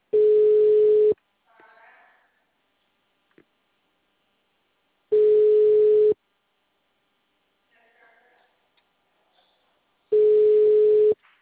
Ringback tone (Europe, 425 Hz).